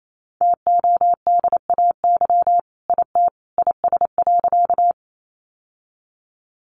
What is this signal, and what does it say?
Morse code, heard as a keyed tone, about 28 wpm, 700 Hz. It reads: TODAY IT IS.
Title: Sentences from Top 600 Words 28wpm